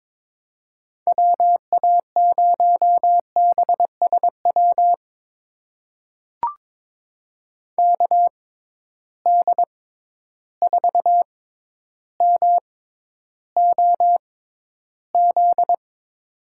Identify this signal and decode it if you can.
Morse code, heard as a keyed tone, about 22 wpm, spaced as for 11 wpm, 700 Hz. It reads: WA0BSW KD4MOZ
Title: USA Call Signs - 6 Characters - 6x Character Spacing - 22wpm